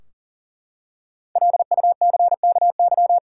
Morse code, heard as a keyed tone, about 39 wpm, 700 Hz. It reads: LUCKY
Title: Five Letter Words - 40wpm